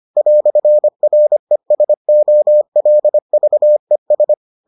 Morse code, heard as a keyed tone, about 25 wpm, 600 Hz. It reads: "RESOLVES